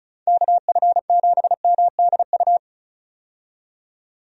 Morse code, heard as a keyed tone, about 36 wpm, 700 Hz. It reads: KF7MDU
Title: USA Call Signs - 6 Characters - 35wpm